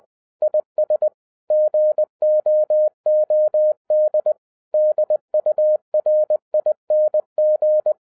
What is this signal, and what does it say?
Morse code, heard as a keyed tone, about 20 wpm, 610 Hz. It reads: IS GOOD DURING